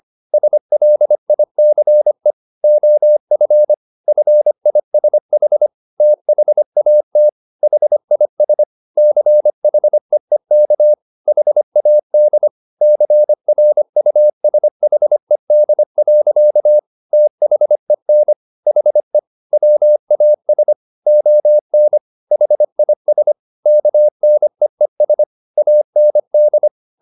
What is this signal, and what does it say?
Morse code, heard as a keyed tone, about 25 wpm, 600 Hz. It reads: SLICE OF FISH THAT HIS CHEEK HAD CRUSHED. THEN HE WAS ON HIS KNEES AND